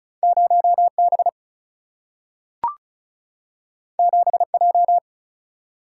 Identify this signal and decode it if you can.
Morse code, heard as a keyed tone, about 36 wpm, 700 Hz. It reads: 0 B 7 J
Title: Single-Letter Single-Number 35wpm